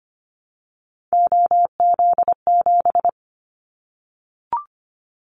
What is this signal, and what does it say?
Morse code, heard as a keyed tone, about 25 wpm, 700 Hz. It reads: OZ7
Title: Callsign Prefixes for Top 50 DXCC - 25wpm